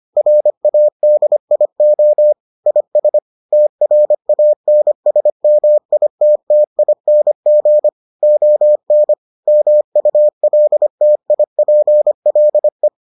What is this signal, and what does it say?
Morse code, heard as a keyed tone, about 25 wpm, 600 Hz. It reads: RADIO IS TRANSMITTING ON MULTIPLE